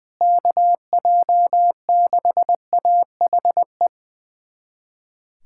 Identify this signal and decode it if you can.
Morse code, heard as a keyed tone, about 20 wpm, 700 Hz. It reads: KJ6AHE
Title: USA Callsign DE Callsign - 6 Characters - 20wpm